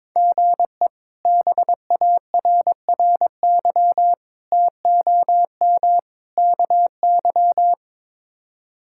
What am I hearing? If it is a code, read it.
Morse code, heard as a keyed tone, about 22 wpm, 700 Hz. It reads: GE BARRY TOM KY
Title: K1USN SST QSO Practice - 22wpm